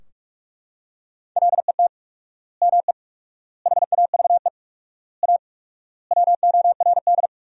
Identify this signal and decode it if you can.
Morse code, heard as a keyed tone, about 44 wpm, 700 Hz. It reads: LET ME HAVE A WORD
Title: 4x Word-Spacing - Sentences from Top 300 Words - 45wpm